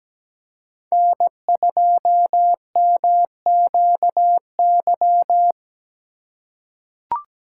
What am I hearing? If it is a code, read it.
Morse code, heard as a keyed tone, about 17 wpm, 700 Hz. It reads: N2MQY
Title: USA Call Signs - 5 Characters - 3x Character Spacing - 17wpm